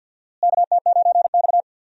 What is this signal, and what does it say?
Morse code, heard as a keyed tone, about 52 wpm, 700 Hz. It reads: KT9X
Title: USA Call Signs - 4 Characters - Repeated 2x - 50wpm